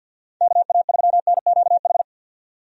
Morse code, heard as a keyed tone, about 46 wpm, 700 Hz. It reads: KA3NQH
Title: USA Call Signs - 6 Characters - 50wpm